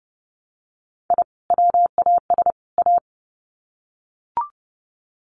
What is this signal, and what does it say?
Morse code, heard as a keyed tone, about 30 wpm, 700 Hz. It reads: I WAS A
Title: Sets of 3 Words 30wpm